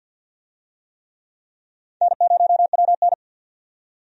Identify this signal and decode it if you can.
Morse code, heard as a keyed tone, about 50 wpm, 700 Hz. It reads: N0WN
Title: USA Call Signs - 4 Characters - 50wpm